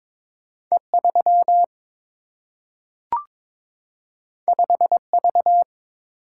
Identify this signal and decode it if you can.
Morse code, heard as a keyed tone, about 22 wpm, 700 Hz. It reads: E 3 5 V
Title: Single-Letter Single-Number 22wpm